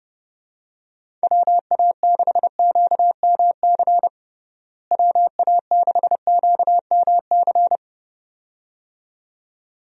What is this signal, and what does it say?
Morse code, heard as a keyed tone, about 30 wpm, 700 Hz. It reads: WA6QMC WA6QMC
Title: USA Call Signs - 6 Characters - Repeated 2x - 30wpm